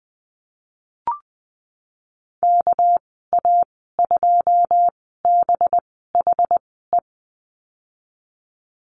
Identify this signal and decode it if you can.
Morse code, heard as a keyed tone, about 20 wpm, spaced as for 14 wpm, 700 Hz. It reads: KA2BHE
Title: 219 Lesson 35g - Number 0 - Call Signs 20wpm